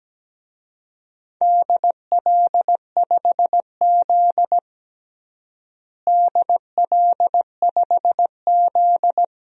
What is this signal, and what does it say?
Morse code, heard as a keyed tone, about 17 wpm, 700 Hz. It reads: DL5Z DL5Z
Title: DXCC-100 Call Signs - 4 Characters - Repeated 2x - 17wpm